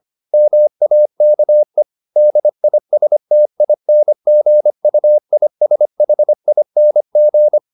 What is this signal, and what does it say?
Morse code, heard as a keyed tone, about 25 wpm, 600 Hz. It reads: MAKE DISTINGUISHING